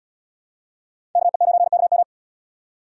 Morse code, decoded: KE9KM

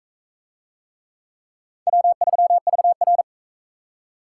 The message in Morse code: W3VR

